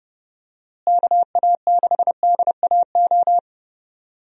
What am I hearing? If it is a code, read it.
Morse code, heard as a keyed tone, about 32 wpm, 700 Hz. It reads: KA6DAO
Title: USA Call Signs - 6 Characters - Repeated 2x - 30wpm